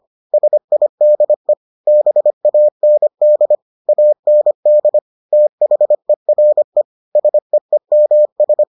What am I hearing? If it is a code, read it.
Morse code, heard as a keyed tone, about 25 wpm, 600 Hz. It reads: SIDE BAND AND THERE SEEMS